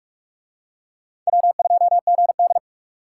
Morse code, heard as a keyed tone, about 42 wpm, 700 Hz. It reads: W2GD